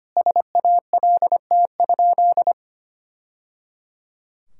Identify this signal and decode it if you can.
Morse code, heard as a keyed tone, about 25 wpm, 700 Hz. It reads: SALT?
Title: Sentences from Top 1000 Words - Review Part 5 - 25wpm